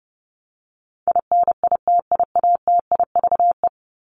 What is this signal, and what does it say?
Morse code, heard as a keyed tone, about 30 wpm, 700 Hz. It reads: INITIATIVE
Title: Top 1751-2000 Words 30wpm